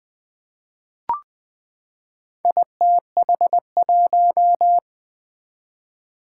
Morse code, heard as a keyed tone, about 20 wpm, 700 Hz. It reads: ITH1
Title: Four Challenging Letter-Number Combinations 20wpm